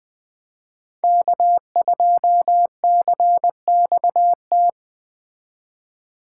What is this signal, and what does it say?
Morse code, heard as a keyed tone, about 19 wpm, 700 Hz. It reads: K2CXT